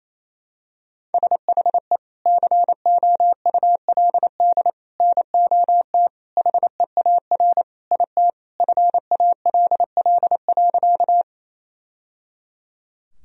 Morse code, heard as a keyed tone, about 28 wpm, 700 Hz. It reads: SHE COULD NOT HEAR IT FALL.
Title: Sentences from Top 1000 Words - Review Part 2 - 28wpm